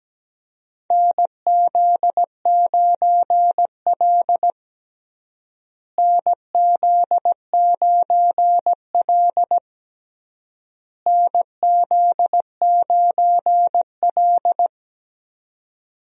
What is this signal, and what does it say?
Morse code, heard as a keyed tone, about 17 wpm, 700 Hz. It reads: NZ9L NZ9L NZ9L
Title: USA Call Signs - 4 Characters - Repeated 3x - 17wpm